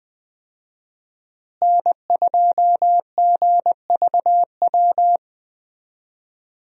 Morse code, heard as a keyed tone, about 20 wpm, 700 Hz. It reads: N2GVW
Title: USA Call Signs - 5 Characters - Repeated 3x - 20wpm